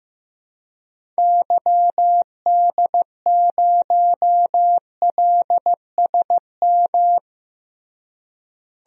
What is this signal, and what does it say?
Morse code, heard as a keyed tone, about 15 wpm, 700 Hz. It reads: YD0LSM
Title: DXCC-100 Call Signs - 6 Characters - Repeated 3x - 15wpm